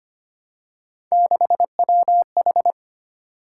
Morse code, heard as a keyed tone, about 25 wpm, 700 Hz. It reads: 6WH
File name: Three Letter-Number Combinations - 25wpm